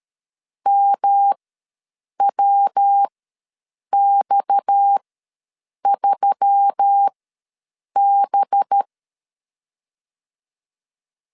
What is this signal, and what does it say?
Morse code, heard as a keyed tone, about 13 wpm, spaced as for 7 wpm, 800 Hz. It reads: MWX3B
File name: LEZIONI / 7 GF9Z / 7 G ws 12wpm 3140